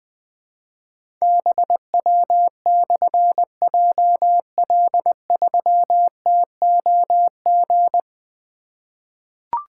Morse code, heard as a keyed tone, about 20 wpm, 700 Hz. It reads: BW/JL3TOG
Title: Difficult Call Signs 20wpm